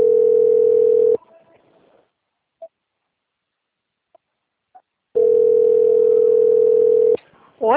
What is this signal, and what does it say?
Ringback tone (North America), 440 + 480 Hz.